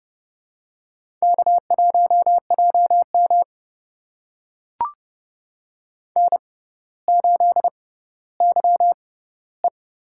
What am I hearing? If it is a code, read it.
Morse code, heard as a keyed tone, about 30 wpm, spaced as for 15 wpm, 700 Hz. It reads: K1JM N8YE
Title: USA Call Signs - 4 Characters - 6x Character Spacing - 30wpm